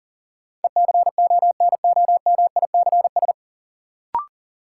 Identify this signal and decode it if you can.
Morse code, heard as a keyed tone, about 40 wpm, 700 Hz. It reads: ECONOMICS
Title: Sentences from Top 1000 Words - Review Part 3 - New Word Drill - 40wpm